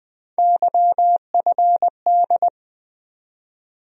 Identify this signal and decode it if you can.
Morse code, heard as a keyed tone, about 20 wpm, 700 Hz. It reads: YFD
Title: USA Call Sign Postfixes - 3 Letters 20wpm